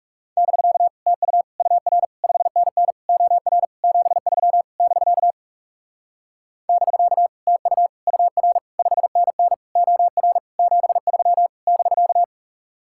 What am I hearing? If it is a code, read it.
Morse code, heard as a keyed tone, about 38 wpm, 700 Hz. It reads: <BK> TU UR 5NN OR 73 <BK> <BK> TU UR 5NN OR 73 <BK>
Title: Speed-Racing Common QSO Phrases - 30wpm